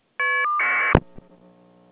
Caller ID data burst.